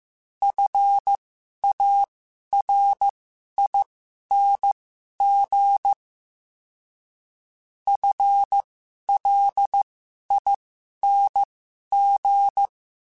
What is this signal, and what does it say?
Morse code, heard as a keyed tone, about 15 wpm, spaced as for 11 wpm, 800 Hz. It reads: FARING FLING
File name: Special INGs-mp3s / Special INGs-mp3s / Sngl Ing @10